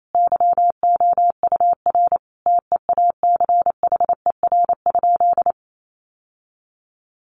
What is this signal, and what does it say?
Morse code, heard as a keyed tone, about 28 wpm, 700 Hz. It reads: YOUR TEACHER?
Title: Sentences from Top 1300 Words 28wpm